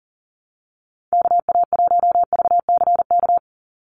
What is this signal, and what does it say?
Morse code, heard as a keyed tone, about 40 wpm, 700 Hz. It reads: KA1VCK
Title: USA Call Signs 40wpm